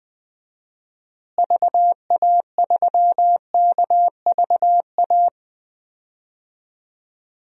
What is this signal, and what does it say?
Morse code, heard as a keyed tone, about 20 wpm, 700 Hz. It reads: VA3KVA
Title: DXCC-100 Call Signs - 6 Characters - Repeated 2x - 20wpm